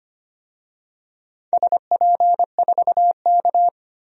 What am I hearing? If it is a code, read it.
Morse code, heard as a keyed tone, about 25 wpm, 700 Hz. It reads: SP4K
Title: DXCC-100 Call Signs - 4 Characters - 25wpm